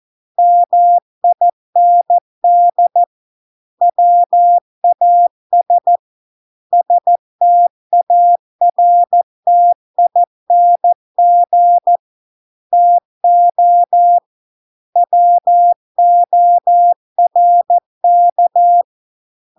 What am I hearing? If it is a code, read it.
Morse code, heard as a keyed tone, about 14 wpm, 700 Hz. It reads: MIND WAS STARTING TO WORK